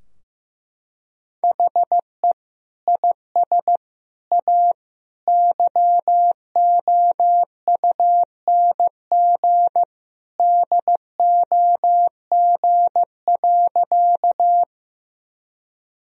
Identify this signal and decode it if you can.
Morse code, heard as a keyed tone, about 15 wpm, 700 Hz. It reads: HE IS A YOUNG DOG.
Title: Sentences from Top 1000 Words - Review Part 2 - 15wpm